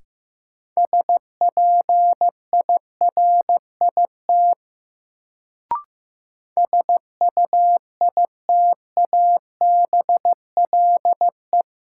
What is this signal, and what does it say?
Morse code, heard as a keyed tone, about 15 wpm, 700 Hz. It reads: SPIRIT SUITABLE
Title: Sentences from Top 1400 Words - New Word Drill Alt - 15wpm